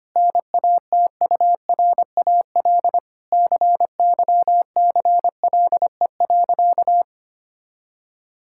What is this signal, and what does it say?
Morse code, heard as a keyed tone, about 25 wpm, 700 Hz. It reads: NATURAL CYCLE.
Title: Sentences from Top 1000 Words - Review Part 6 - 25wpm